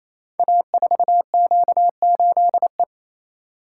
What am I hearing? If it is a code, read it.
Morse code, heard as a keyed tone, about 27 wpm, 700 Hz. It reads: A4Q8E
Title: Code Groups - 5 Letter:Numbers - 28wpm